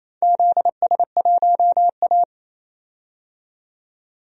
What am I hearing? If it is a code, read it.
Morse code, heard as a keyed tone, about 28 wpm, 700 Hz. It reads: ZS1A
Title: DXCC-100 Call Signs - 4 Characters - 28wpm